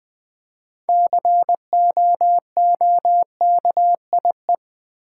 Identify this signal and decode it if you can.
Morse code, heard as a keyed tone, about 20 wpm, 700 Hz. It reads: COOKIE
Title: Six Letter Words - 20wpm